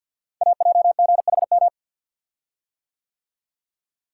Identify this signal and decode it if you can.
Morse code, heard as a keyed tone, about 47 wpm, 700 Hz. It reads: AJGHM
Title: Code Groups - 5 Letters - 50wpm